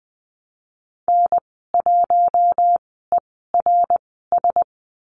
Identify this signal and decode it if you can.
Morse code, heard as a keyed tone, about 20 wpm, spaced as for 15 wpm, 700 Hz. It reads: N1ERS
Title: 062 Lesson 11g - Letter h - Call Signs 20wpm